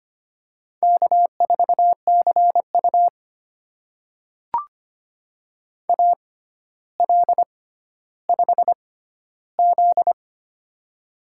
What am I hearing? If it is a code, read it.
Morse code, heard as a keyed tone, about 25 wpm, spaced as for 13 wpm, 700 Hz. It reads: K4CU AL5Z
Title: USA Call Signs - 4 Characters - 6x Character Spacing - 25wpm